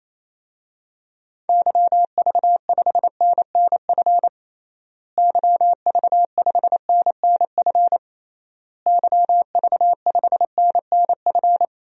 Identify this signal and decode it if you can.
Morse code, heard as a keyed tone, about 27 wpm, 700 Hz. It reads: YV5NNF YV5NNF YV5NNF
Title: DXCC-100 Call Signs - 6 Characters - Repeated 3x - 28wpm